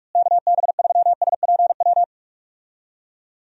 Morse code, heard as a keyed tone, about 43 wpm, 700 Hz. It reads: KB3SPW
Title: USA Call Signs - 6 Characters - Repeated 3x - 45wpm